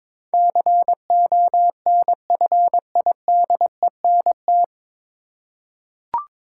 Morse code, heard as a keyed tone, about 22 wpm, 700 Hz. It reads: CONFIDENT
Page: Morse Code Ninja practice 22wpm